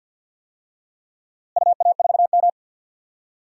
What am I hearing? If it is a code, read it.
Morse code, heard as a keyed tone, about 47 wpm, 700 Hz. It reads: UA4M
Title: DXCC-100 Call Signs - 4 Characters - Repeated 2x - 50wpm